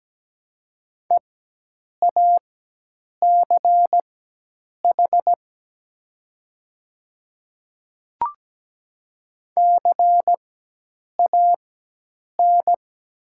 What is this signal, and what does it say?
Morse code, heard as a keyed tone, about 17 wpm, spaced as for 8 wpm, 700 Hz. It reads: EACH CAN
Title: 4x Character Spacing - Top 100 Words - 17wpm